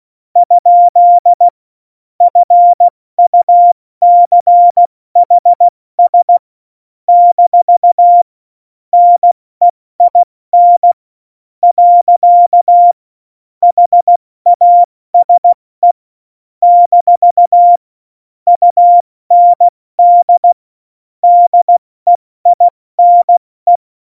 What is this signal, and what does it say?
Morse code, heard as a keyed tone, about 16 wpm, 700 Hz. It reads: ? FUCHS - NEIN . HASE - UND DEINE
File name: Morsekurs / Witze / Witz 28. August 2018, 70 F80